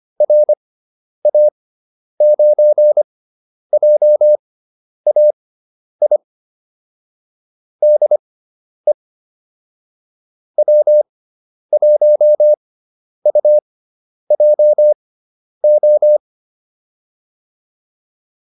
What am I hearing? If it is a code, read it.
Morse code, heard as a keyed tone, about 25 wpm, spaced as for 10 wpm, 600 Hz. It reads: RA9JAI DE W1UJO
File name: QSO401 10